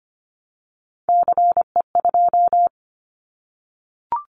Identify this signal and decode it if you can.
Morse code, heard as a keyed tone, about 25 wpm, 700 Hz. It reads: CE2
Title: Callsign Prefixes for Top 100 DXCC - 25wpm